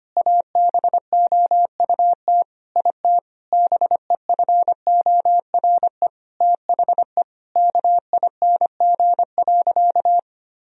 Morse code, heard as a keyed tone, about 25 wpm, 700 Hz. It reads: ABOUT IT BEFORE THE KING.
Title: Sentences from Top 1000 Words - Review Part 3 - 25wpm